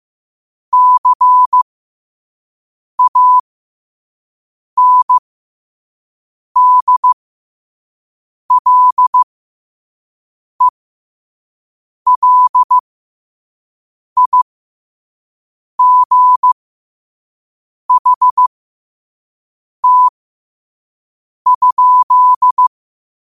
Morse code, heard as a keyed tone, about 15 wpm, spaced as for 5 wpm, 1000 Hz. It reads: CANDLELIGHT?
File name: qotdfiles / 5wpm